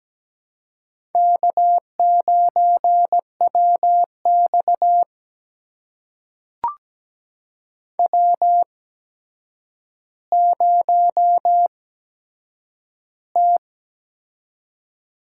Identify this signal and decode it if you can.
Morse code, heard as a keyed tone, about 17 wpm, spaced as for 8 wpm, 700 Hz. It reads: K9WX W0T
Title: USA Call Signs - 4 Characters - 8x Character Spacing - 17wpm